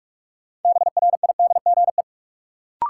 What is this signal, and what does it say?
Morse code, heard as a keyed tone, about 46 wpm, 700 Hz. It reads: BRIDGE
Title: Sentences from Top 1200 Words - New Word Drill Alt - 45wpm